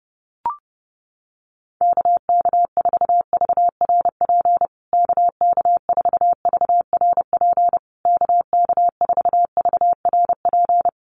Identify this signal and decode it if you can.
Morse code, heard as a keyed tone, about 30 wpm, 700 Hz. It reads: KK4VRP KK4VRP KK4VRP
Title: USA Call Signs Repeated 3x 30wpm